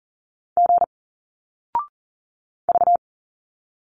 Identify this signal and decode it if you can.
Morse code, heard as a keyed tone, about 40 wpm, 700 Hz. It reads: G V
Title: Single Letters 40wpm